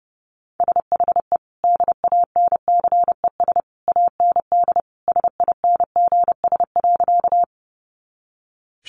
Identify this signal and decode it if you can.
Morse code, heard as a keyed tone, about 30 wpm, 700 Hz. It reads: SHE DANCES AND SINGS.
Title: Easy Phrases 30wpm